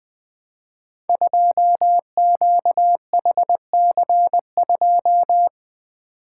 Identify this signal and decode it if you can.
Morse code, heard as a keyed tone, about 21 wpm, 700 Hz. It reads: 2QHC2
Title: Code Groups - 5 Letter:Numbers - 20wpm